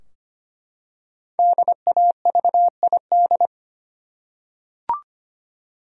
Morse code, heard as a keyed tone, about 25 wpm, 700 Hz. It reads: DAVID
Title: USA Common Names - Max 5 Letters - 25wpm